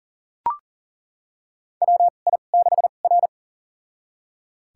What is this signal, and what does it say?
Morse code, heard as a keyed tone, about 37 wpm, spaced as for 28 wpm, 700 Hz. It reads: WI6R
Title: USA Call Signs - 4 Characters - 2x Character Spacing - 40wpm